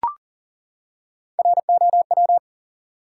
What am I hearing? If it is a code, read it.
Morse code, heard as a keyed tone, about 40 wpm, 700 Hz. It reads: ROW
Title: Three Letter Words - 40wpm